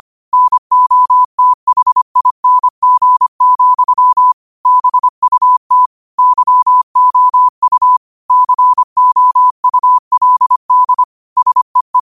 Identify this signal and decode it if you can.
Morse code, heard as a keyed tone, about 25 wpm, 1000 Hz. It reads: NOTHING, BUT YOU COULD SEE